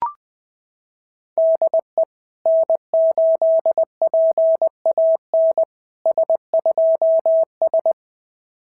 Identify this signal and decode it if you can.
Morse code, heard as a keyed tone, about 20 wpm, 660 Hz. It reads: DE N8PAN S2S
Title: SOTA QSO Practice 20wpm